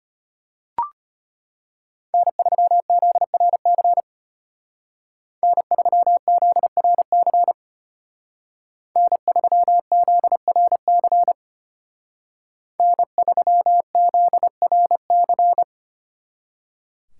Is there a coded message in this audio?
Yes: Morse code from 2.1 to 15.6 s, heard as a keyed tone, about 32 wpm, 700 Hz. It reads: N3ZRC N3ZRC N3ZRC N3ZRC